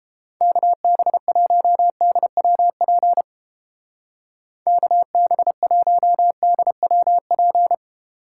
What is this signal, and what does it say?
Morse code, heard as a keyed tone, about 29 wpm, 700 Hz. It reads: KB1DWP KB1DWP